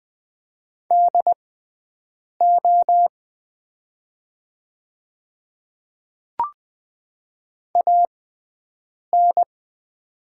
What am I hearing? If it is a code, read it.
Morse code, heard as a keyed tone, about 20 wpm, spaced as for 7 wpm, 700 Hz. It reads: DO AN
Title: 6x Character Spacing - Top 100 Words - 20wpm